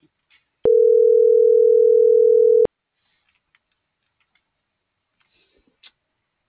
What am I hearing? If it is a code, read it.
Ringback tone (North America), 440 + 480 Hz.